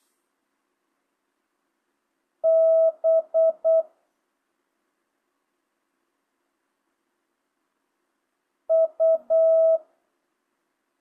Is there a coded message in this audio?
Yes: Morse code from 2.4 to 9.8 s, heard as a keyed tone, about 8 wpm, 650 Hz. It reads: B U